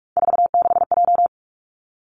Morse code, heard as a keyed tone, about 46 wpm, 700 Hz. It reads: 46J
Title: Three Challenging Letter-Number Combinations 45wpm